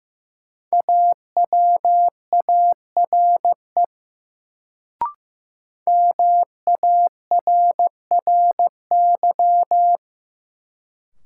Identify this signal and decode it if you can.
Morse code, heard as a keyed tone, about 15 wpm, 700 Hz. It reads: AWARE MARRY
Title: Sentences from Top 1200 Words - New Word Drill - 15wpm